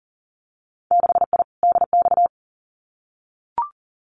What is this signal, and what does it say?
Morse code, heard as a keyed tone, about 40 wpm, 700 Hz. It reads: 6I DX
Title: Winter Field Day Exchanges Encore 40wpm